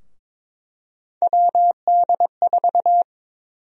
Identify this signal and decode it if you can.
Morse code, heard as a keyed tone, about 22 wpm, 700 Hz. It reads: WD4
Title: USA Call Sign Prefixes - Repeated 2x - 22wpm